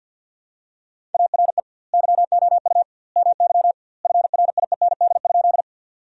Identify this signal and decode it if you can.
Morse code, heard as a keyed tone, about 50 wpm, 700 Hz. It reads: ARE YOU MY FRIEND?
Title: 2x Word-Spacing - Sentences from Top 300 Words - 50wpm